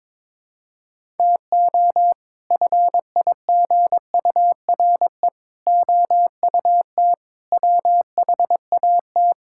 Morse code, heard as a keyed tone, about 22 wpm, 700 Hz. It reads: TO FIGURE OUT WHAT
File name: practice / 22wpm / Sets of 6 Words - 22wpm